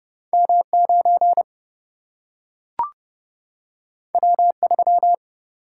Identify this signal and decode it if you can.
Morse code, heard as a keyed tone, about 30 wpm, 700 Hz. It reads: M9 W3
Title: Single-Letter Single-Number 30wpm